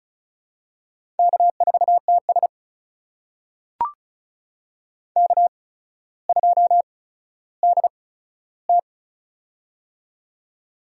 Morse code, heard as a keyed tone, about 36 wpm, spaced as for 15 wpm, 700 Hz. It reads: K4TS K2DT
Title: USA Call Signs - 4 Characters - 8x Character Spacing - 35wpm